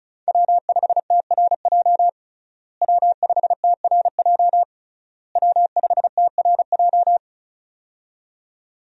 Morse code, heard as a keyed tone, about 36 wpm, 700 Hz. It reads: W5TRJ W5TRJ W5TRJ
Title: USA Call Signs - 5 Characters - Repeated 3x - 35wpm